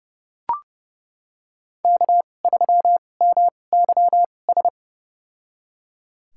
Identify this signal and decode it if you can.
Morse code, heard as a keyed tone, about 30 wpm, spaced as for 22 wpm, 700 Hz. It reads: K3MYS
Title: USA Call Signs - 5 Characters - 2x Character Spacing - 30wpm